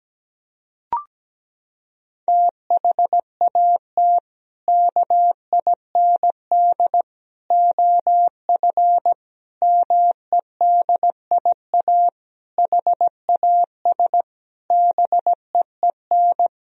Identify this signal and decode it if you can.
Morse code, heard as a keyed tone, about 17 wpm, 700 Hz. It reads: THAT KIND OF MEDIA HAS BEEN
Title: Sentences from Top 1000 Words - Review Part 4 - 17wpm